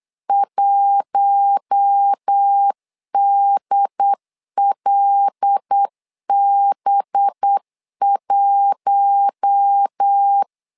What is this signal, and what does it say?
Morse code, heard as a keyed tone, about 8 wpm, 800 Hz. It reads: 1DLB1
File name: LEZIONI / 5 LBVC / 5 LBV 8wpm 2837